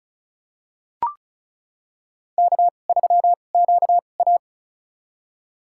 Morse code, heard as a keyed tone, about 35 wpm, spaced as for 25 wpm, 700 Hz. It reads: K3QA
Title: USA Call Signs - 4 Characters - 2x Character Spacing - 35wpm